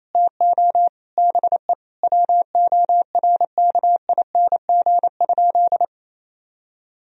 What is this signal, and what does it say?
Morse code, heard as a keyed tone, about 28 wpm, 700 Hz. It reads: TO BE WORKING?
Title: Sentences from Top 1000 Words - Review Part 4 - 28wpm